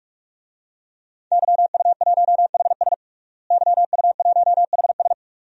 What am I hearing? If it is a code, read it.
Morse code, heard as a keyed tone, about 41 wpm, 700 Hz. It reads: YU1HS YU1HS
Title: DXCC-100 Call Signs - 5 Characters - Repeated 2x - 45wpm